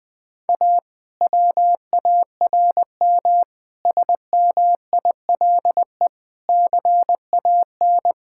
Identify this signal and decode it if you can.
Morse code, heard as a keyed tone, about 20 wpm, 700 Hz. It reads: A WARM SMILE CAN
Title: Difficult Sentences with Hints - 20wpm